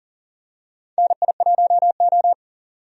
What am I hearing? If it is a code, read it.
Morse code, heard as a keyed tone, about 38 wpm, 700 Hz. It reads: NI1O
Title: USA Call Signs - 4 Characters - 40wpm